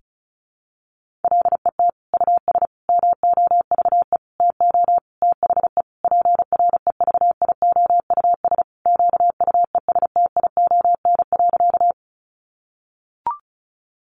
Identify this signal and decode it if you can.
Morse code, heard as a keyed tone, about 34 wpm, 700 Hz. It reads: LET US MOVE TO THE PREVIOUS QUESTION.
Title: Sentences from Top 1500 Words 35wpm